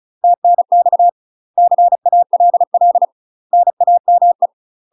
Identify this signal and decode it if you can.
Morse code, heard as a keyed tone, about 36 wpm, 700 Hz. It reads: TNX CALL NAME
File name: UK MASTER QSOs TRAINING / 35WPM / UK MASTER QSOs 0023